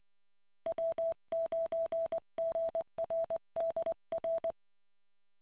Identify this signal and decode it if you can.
Morse code, heard as a keyed tone, about 20 wpm, 660 Hz. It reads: W9GRD R